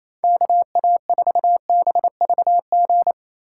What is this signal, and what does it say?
Morse code, heard as a keyed tone, about 28 wpm, 700 Hz. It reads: KA4BVG
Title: K1USN SST QSO Practice - 28wpm